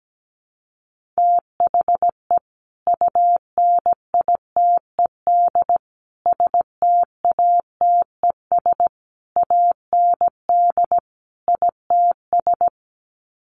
Morse code, heard as a keyed tone, about 17 wpm, 700 Hz. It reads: THE UNITED STATES AND ITS
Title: Sets of 5 Words Encore 17wpm